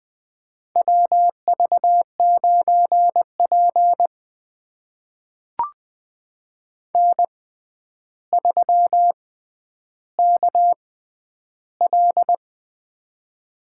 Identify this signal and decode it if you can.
Morse code, heard as a keyed tone, about 20 wpm, spaced as for 10 wpm, 700 Hz. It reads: WV9P N3KL
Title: USA Call Signs - 4 Characters - 6x Character Spacing - 20wpm